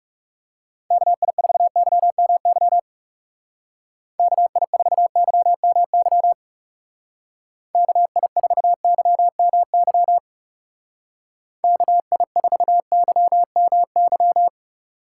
Morse code, heard as a keyed tone, about 37 wpm, 700 Hz. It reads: KI4YMY KI4YMY KI4YMY KI4YMY
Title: Speed Racing - USA 2x3 Call Signs 30wpm